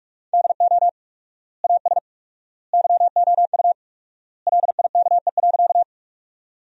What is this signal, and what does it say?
Morse code, heard as a keyed tone, about 46 wpm, 700 Hz. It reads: DO AS YOU LIKE.
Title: 4x Word-Spacing - Sentences from Top 100 Words - 45wpm